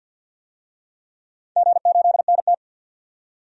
Morse code, heard as a keyed tone, about 53 wpm, 700 Hz. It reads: G8NT
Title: DXCC-100 Call Signs - 4 Characters - 50wpm